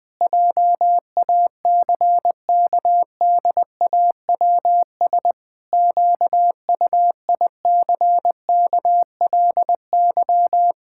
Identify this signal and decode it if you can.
Morse code, heard as a keyed tone, about 20 wpm, 700 Hz. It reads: JACKDAWS QUICKLY
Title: Pangram Challenge v2 - 20wpm